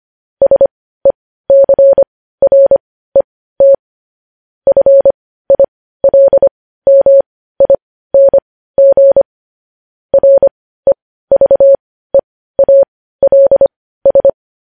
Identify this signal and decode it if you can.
Morse code, heard as a keyed tone, about 25 wpm, spaced as for 15 wpm, 550 Hz. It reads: SECRET FILMING REVEALS
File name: podcast / News headlines-1x-25x15